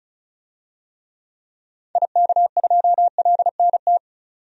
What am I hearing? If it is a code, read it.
Morse code, heard as a keyed tone, about 36 wpm, 700 Hz. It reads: IK2LNT
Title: DXCC-100 Call Signs - 6 Characters - 35wpm